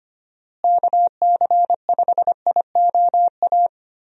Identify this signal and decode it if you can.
Morse code, heard as a keyed tone, about 25 wpm, 700 Hz. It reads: KC5IOA
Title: USA Call Signs - 6 Characters - Repeated 3x - 25wpm